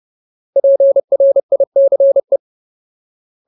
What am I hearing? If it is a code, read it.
Morse code, heard as a keyed tone, about 30 wpm, 550 Hz. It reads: PRICE
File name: Ultimate Word List / 30WPM Ultimate